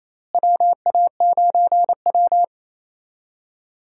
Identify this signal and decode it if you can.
Morse code, heard as a keyed tone, about 28 wpm, 700 Hz. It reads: WA9W
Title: USA Call Signs - 4 Characters - Repeated 3x - 28wpm